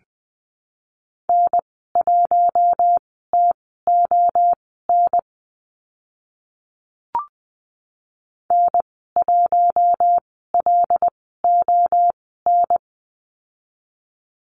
Morse code, heard as a keyed tone, about 20 wpm, spaced as for 14 wpm, 700 Hz. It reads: N1TON N1LON
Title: 076 Lesson 13g - Letter L - Call Signs 20wpm